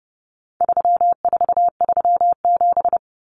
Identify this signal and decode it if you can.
Morse code, heard as a keyed tone, about 30 wpm, 700 Hz. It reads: 3437
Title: Common Cut Numbers - 4-Digit 30wpm